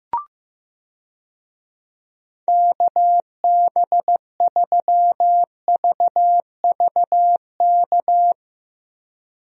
Morse code, heard as a keyed tone, about 15 wpm, 700 Hz. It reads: KB3VVK
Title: USA Call Signs - 6 Characters - Repeated 3x - 15wpm